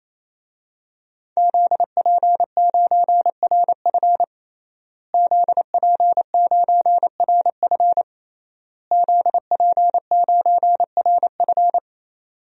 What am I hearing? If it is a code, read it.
Morse code, heard as a keyed tone, about 28 wpm, 700 Hz. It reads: ZP9RF ZP9RF ZP9RF
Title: DXCC-100 Call Signs - 5 Characters - Repeated 3x - 28wpm